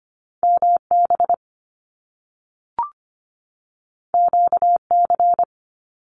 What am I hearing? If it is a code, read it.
Morse code, heard as a keyed tone, about 25 wpm, 700 Hz. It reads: MB QC
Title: Canadian Provinces and Territories 25wpm